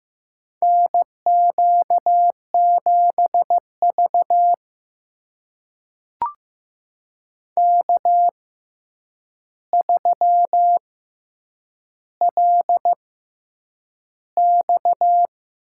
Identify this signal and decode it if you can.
Morse code, heard as a keyed tone, about 15 wpm, spaced as for 8 wpm, 700 Hz. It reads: NQ7V K3LX
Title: USA Call Signs - 4 Characters - 6x Character Spacing - 15wpm